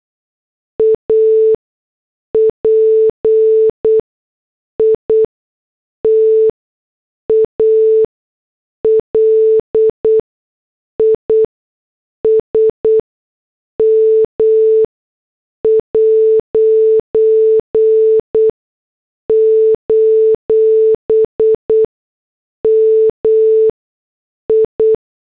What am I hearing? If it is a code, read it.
Morse code, heard as a keyed tone, about 8 wpm, spaced as for 6 wpm, 440 Hz. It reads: APITALISM': MI